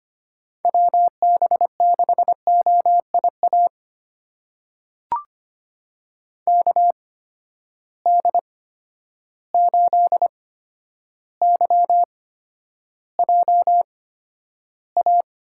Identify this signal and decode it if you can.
Morse code, heard as a keyed tone, about 25 wpm, spaced as for 11 wpm, 700 Hz. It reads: WB6OIA KD8YJA
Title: USA Call Signs - 8x Character Spacing - 25wpm